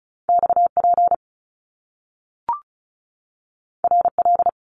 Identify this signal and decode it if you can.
Morse code, heard as a keyed tone, about 36 wpm, 700 Hz. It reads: XP RL
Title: Top Two Letter Combinations in Words 35wpm